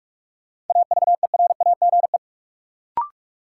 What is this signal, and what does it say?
Morse code, heard as a keyed tone, about 48 wpm, 700 Hz. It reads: AVERAGE
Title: Sets of 5 Words - New Word Drill - 45wpm